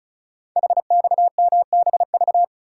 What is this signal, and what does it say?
Morse code, heard as a keyed tone, about 36 wpm, 690 Hz. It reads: HXMBV
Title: Code Groups - 5 Letters - 35wpm